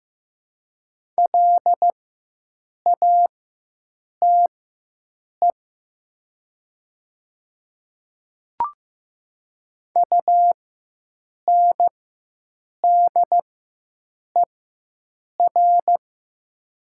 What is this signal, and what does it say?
Morse code, heard as a keyed tone, about 15 wpm, spaced as for 7 wpm, 700 Hz. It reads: LATE UNDER
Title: 4x Character Spacing - Top 300 Words - 15wpm